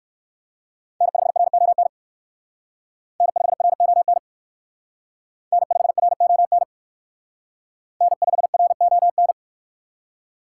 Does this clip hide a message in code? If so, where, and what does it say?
Yes: Morse code from 1.0 to 9.3 s, heard as a keyed tone, about 57 wpm, 700 Hz. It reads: N5RON N5RON N5RON N5RON